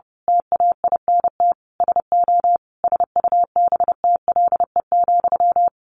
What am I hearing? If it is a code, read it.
Morse code, heard as a keyed tone, about 30 wpm, 700 Hz. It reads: TAINT SO SUBTLE,
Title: A Sound Of Thunder 30wpm